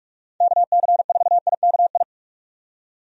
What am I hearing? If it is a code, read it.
Morse code, heard as a keyed tone, about 46 wpm, 700 Hz. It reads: KC4IKI
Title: USA Call Signs - 6 Characters - 45wpm